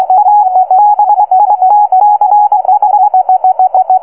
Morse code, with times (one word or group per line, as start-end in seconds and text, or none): 0.2-0.4 A
0.8-3.1 BITCH